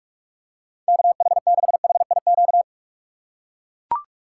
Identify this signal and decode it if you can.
Morse code, heard as a keyed tone, about 43 wpm, 700 Hz. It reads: KH6HIQ
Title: USA Call Signs - 6 Characters - 6x Character Spacing - 45wpm